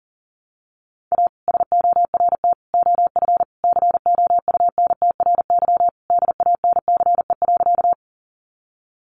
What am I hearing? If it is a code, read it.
Morse code, heard as a keyed tone, about 40 wpm, 700 Hz. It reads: A SORT OF COUNTRY DANCE.
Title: Sentences from Top 1300 Words 40wpm